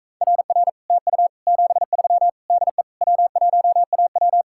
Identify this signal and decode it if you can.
Morse code, heard as a keyed tone, about 45 wpm, 700 Hz. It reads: RR TU 73 DE W1AW